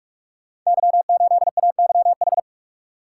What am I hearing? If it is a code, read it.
Morse code, heard as a keyed tone, about 47 wpm, 700 Hz. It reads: Y8AYH
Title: Code Groups - 5 Letter:Numbers - 45wpm